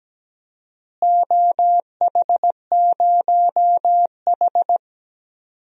Morse code, heard as a keyed tone, about 17 wpm, 700 Hz. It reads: OH0H
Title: DXCC-100 Call Signs - 4 Characters - Repeated 2x - 17wpm